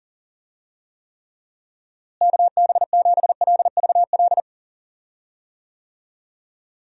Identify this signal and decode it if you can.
Morse code, heard as a keyed tone, about 39 wpm, 700 Hz. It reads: KB7LVL
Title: USA Call Signs - 6 Characters - Repeated 3x - 40wpm